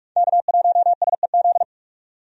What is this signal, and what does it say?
Morse code, heard as a keyed tone, about 42 wpm, 700 Hz. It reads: K1SEZ
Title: USA Call Signs - 5 Characters - 45wpm